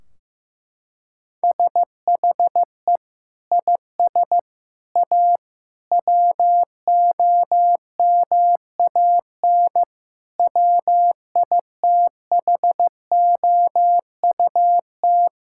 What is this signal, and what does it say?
Morse code, heard as a keyed tone, about 16 wpm, 700 Hz. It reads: SHE IS A WOMAN WITHOUT
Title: Sentences from Top 1600 Words 15wpm